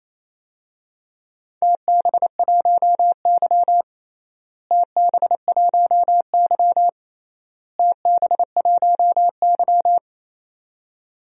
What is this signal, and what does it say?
Morse code, heard as a keyed tone, about 28 wpm, 700 Hz. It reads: TB1Y TB1Y TB1Y
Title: DXCC-100 Call Signs - 4 Characters - Repeated 3x - 28wpm